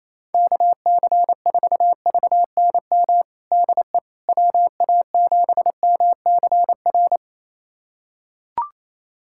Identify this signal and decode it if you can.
Morse code, heard as a keyed tone, about 28 wpm, 700 Hz. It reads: KC4VNM DE WA7MCR
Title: USA Callsign DE Callsign - 6 Characters - 28wpm